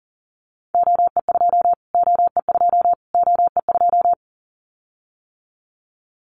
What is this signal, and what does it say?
Morse code, heard as a keyed tone, about 40 wpm, 700 Hz. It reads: OE2 OE2 OE2
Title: Callsign Prefixes for Top 25 DXCC - Repeated 3x - 40wpm